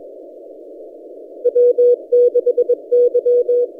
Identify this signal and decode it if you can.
Morse code, heard as a keyed tone, about 21 wpm, 490 Hz. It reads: W6Y
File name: posts / a35rt-10cw- -0134z